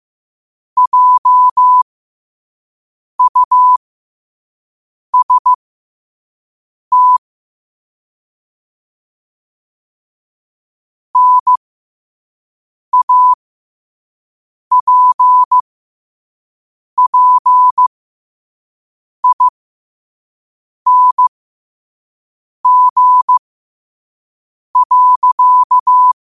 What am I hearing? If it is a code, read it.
Morse code, heard as a keyed tone, about 15 wpm, spaced as for 5 wpm, 1000 Hz. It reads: JUST NAPPING.
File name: 5wpm